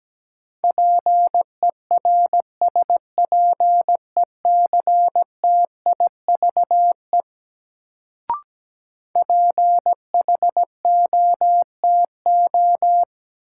Morse code, heard as a keyed tone, about 17 wpm, 700 Hz. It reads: PERSPECTIVE PHOTO
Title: Sentences from Top 1200 Words - New Word Drill Alt - 17wpm